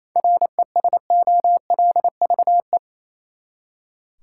Morse code, heard as a keyed tone, about 28 wpm, 700 Hz. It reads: RESOLVE
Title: Seven Letter Words - 28wpm